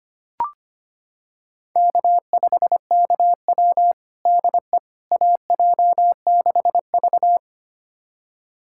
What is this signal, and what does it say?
Morse code, heard as a keyed tone, about 25 wpm, 700 Hz. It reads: K5KW DE AJ6V
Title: USA Callsign DE Callsign - 4 Characters - 25wpm